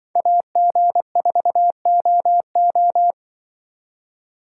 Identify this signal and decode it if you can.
Morse code, heard as a keyed tone, about 25 wpm, 700 Hz. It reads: AG4OO
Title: Speed Racing - USA 2x2 Call Signs 20wpm